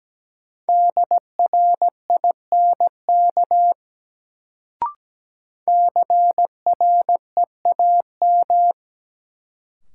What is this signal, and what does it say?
Morse code, heard as a keyed tone, about 17 wpm, 700 Hz. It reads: DRINK CREAM
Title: Sentences from Top 1500 Words - New Word Drill - 17wpm